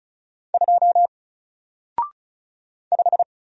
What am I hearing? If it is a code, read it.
Morse code, heard as a keyed tone, about 35 wpm, 700 Hz. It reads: 2 5
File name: Single-Letter-Number-35wpm